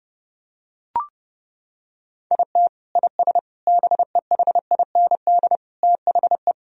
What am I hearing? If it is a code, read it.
Morse code, heard as a keyed tone, about 30 wpm, 700 Hz. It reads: IT IS BEHIND THE
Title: Sentences from Top 1000 Words - Review Part 1 - 30wpm